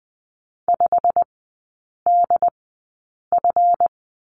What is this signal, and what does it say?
Morse code, heard as a keyed tone, about 20 wpm, spaced as for 8 wpm, 690 Hz. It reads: 5DF
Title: 145 Lesson 24e - Letter y - Three Characters 20wpm